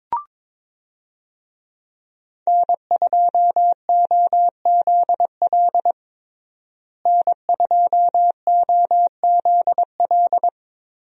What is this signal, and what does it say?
Morse code, heard as a keyed tone, about 22 wpm, 700 Hz. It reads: N2OZL N2OZL
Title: USA Call Signs - 5 Characters - Repeated 3x - 22wpm